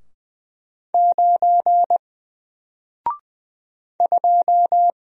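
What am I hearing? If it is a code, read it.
Morse code, heard as a keyed tone, about 20 wpm, 700 Hz. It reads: 9 2
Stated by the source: Single letter and number Morse drill at 20 wpm, from N4LCD.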